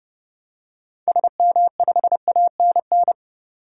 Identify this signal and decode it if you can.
Morse code, heard as a keyed tone, about 30 wpm, 700 Hz. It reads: SM5ANN
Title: DXCC-100 Call Signs - 6 Characters - 30wpm